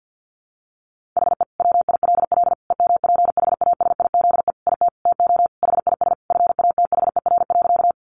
Morse code, heard as a keyed tone, about 48 wpm, 700 Hz. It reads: HE WILL EMPHASIZE IT TO HIS FATHER.
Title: Sentences from Top 1500 Words 50wpm